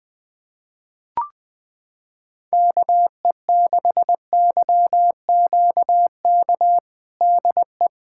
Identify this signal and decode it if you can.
Morse code, heard as a keyed tone, about 20 wpm, 700 Hz. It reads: KE6YQK DE
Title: USA Callsign DE Callsign 20wpm